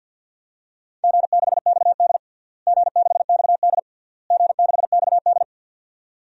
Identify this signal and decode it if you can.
Morse code, heard as a keyed tone, about 51 wpm, 700 Hz. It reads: G6XD G6XD G6XD
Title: DXCC-100 Call Signs - 4 Characters - Repeated 3x - 50wpm